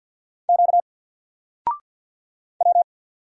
Morse code, heard as a keyed tone, about 50 wpm, 700 Hz. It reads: Q W
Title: Single Letter-Number 50wpm